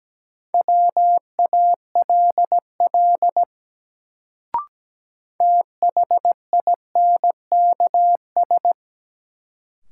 Morse code, heard as a keyed tone, about 17 wpm, 700 Hz. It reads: WALL THINKS
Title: Sentences from Top 1000 Words - Review Part 6 - New Word Drill - 17wpm